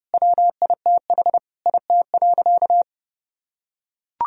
Morse code, heard as a keyed tone, about 30 wpm, 700 Hz. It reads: WITH IT.
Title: Sentences from Top 1000 Words - Review Part 3 - 30wpm